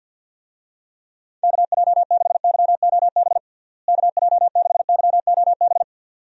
Morse code, heard as a keyed tone, about 50 wpm, 700 Hz. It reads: KJ6YOB KJ6YOB
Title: USA Call Signs - 6 Characters - Repeated 2x - 50wpm